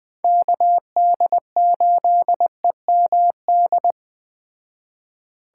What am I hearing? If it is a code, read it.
Morse code, heard as a keyed tone, about 20 wpm, 700 Hz. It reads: KD8EMD
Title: Speed Racing - USA Call Signs 17wpm